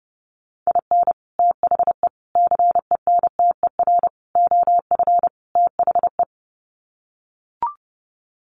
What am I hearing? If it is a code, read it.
Morse code, heard as a keyed tone, about 28 wpm, 700 Hz. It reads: IN THE CENTER OF THE
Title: Sets of 5 Words 30wpm